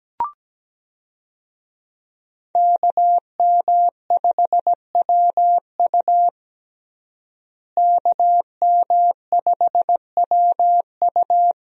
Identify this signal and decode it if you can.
Morse code, heard as a keyed tone, about 17 wpm, 700 Hz. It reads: KM5WU KM5WU
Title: USA Call Signs - 5 Characters - Repeated 2x - 17wpm